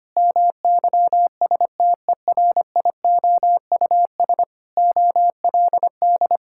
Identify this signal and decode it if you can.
Morse code, heard as a keyed tone, about 25 wpm, 700 Hz. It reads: MYSTERIOUS OLD
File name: practice / 25wpm / Sentences with 2-Word Phrases - 25wpm